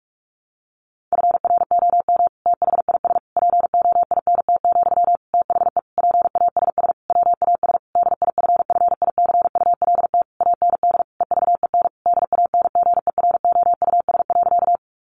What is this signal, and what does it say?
Morse code, heard as a keyed tone, about 44 wpm, 700 Hz. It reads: FROM THIS POINT, THE PASS WAS DIFFICULT AND EVEN DANGEROUS.
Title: Sentences from Top 1200 Words 45wpm